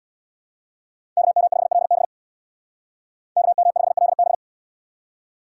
Morse code, heard as a keyed tone, about 77 wpm, 700 Hz. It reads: KM5LB KM5LB